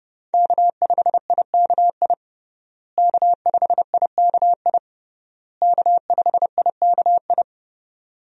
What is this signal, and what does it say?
Morse code, heard as a keyed tone, about 32 wpm, 700 Hz. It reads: K5IKI K5IKI K5IKI
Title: USA Call Signs - 5 Characters - Repeated 3x - 30wpm